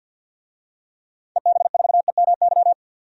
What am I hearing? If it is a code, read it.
Morse code, heard as a keyed tone, about 47 wpm, 700 Hz. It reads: EB4EMY
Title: DXCC-100 Call Signs - 6 Characters - 50wpm